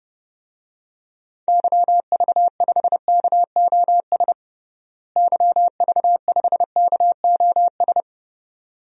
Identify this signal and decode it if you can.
Morse code, heard as a keyed tone, about 30 wpm, 700 Hz. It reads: YV5KOS YV5KOS
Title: DXCC-100 Call Signs - 6 Characters - Repeated 2x - 30wpm